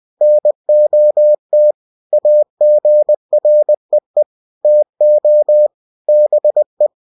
Morse code, heard as a keyed tone, about 20 wpm, 600 Hz. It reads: NOT AGREE TO BE
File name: cw-books / winnie-the-pooh / winnie-20wpm-page-0113